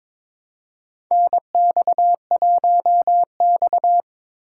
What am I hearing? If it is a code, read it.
Morse code, heard as a keyed tone, about 22 wpm, 700 Hz. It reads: NX1X
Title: USA Call Signs - 4x Character Spacing - 22wpm